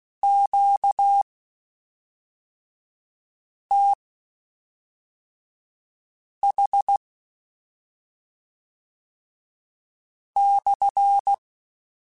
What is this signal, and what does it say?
Morse code, heard as a keyed tone, about 16 wpm, spaced as for 3 wpm, 800 Hz. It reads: QTH /
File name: Q Calls / Q Calls-mp3s / Q Calls SPSL2 @5